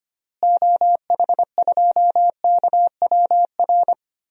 Morse code, heard as a keyed tone, about 25 wpm, 700 Hz. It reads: OH2KWR
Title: DXCC-100 Call Signs - 6 Characters - Repeated 2x - 25wpm